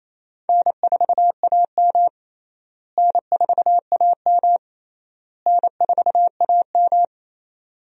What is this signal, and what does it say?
Morse code, heard as a keyed tone, about 28 wpm, 700 Hz. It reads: N4AM N4AM N4AM
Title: USA Call Signs - 4 Characters - Repeated 3x - 28wpm